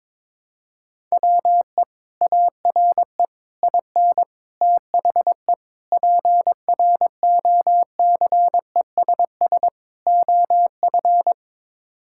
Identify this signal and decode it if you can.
Morse code, heard as a keyed tone, about 21 wpm, 700 Hz. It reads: WE ARE IN THE PROCESS OF
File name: practice / 22wpm / Sets of 6 Words - 22wpm